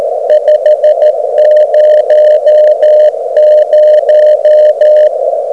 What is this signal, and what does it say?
Morse code, heard as a keyed tone, about 13 wpm, 610 Hz, with radio noise. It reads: 500
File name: Beacon recordings / OK0EF